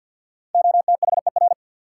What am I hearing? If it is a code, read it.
Morse code, heard as a keyed tone, about 52 wpm, 700 Hz. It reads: OTHER
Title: Sets of 2 Words - New Word Drill - 50wpm